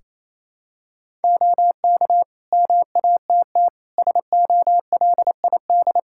Morse code, heard as a keyed tone, about 27 wpm, 700 Hz. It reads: OK MATT SOLID
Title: Practice QSO - Exchange 3 - More Info - 28wpm